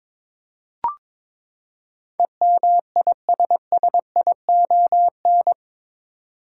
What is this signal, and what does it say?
Morse code, heard as a keyed tone, about 22 wpm, 700 Hz. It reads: EMISSION
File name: practice / 22wpm / Eight Letter Words - 22wpm